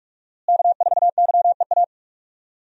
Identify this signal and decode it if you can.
Morse code, heard as a keyed tone, about 46 wpm, 700 Hz. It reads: K4YEA
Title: USA Call Signs - 5 Characters - Repeated 3x - 45wpm